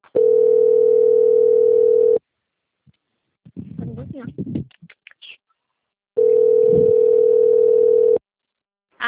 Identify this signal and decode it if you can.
Ringback tone (North America), 440 + 480 Hz, with speech.
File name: out- - - - - .13750